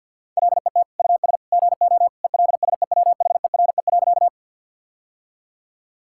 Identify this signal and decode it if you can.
Morse code, heard as a keyed tone, about 50 wpm, 700 Hz. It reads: LET US GO ELSEWHERE.
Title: Sentences from Top 1600 Words 50wpm